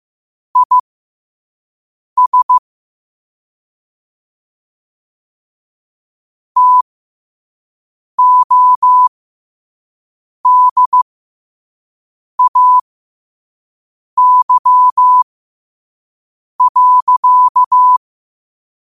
Morse code, heard as a keyed tone, about 15 wpm, spaced as for 5 wpm, 1000 Hz. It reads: IS TODAY.
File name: qotdfiles / 5wpm